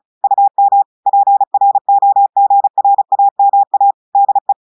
Morse code, heard as a keyed tone, about 36 wpm, 800 Hz. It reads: UM PROGRAMA DE
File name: portugais texte en clair 2 vitesse 35 mots minutes